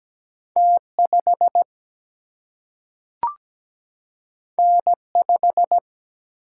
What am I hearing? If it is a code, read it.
Morse code, heard as a keyed tone, about 17 wpm, 700 Hz. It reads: T5 N5
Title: Single-Letter Single-Number 17wpm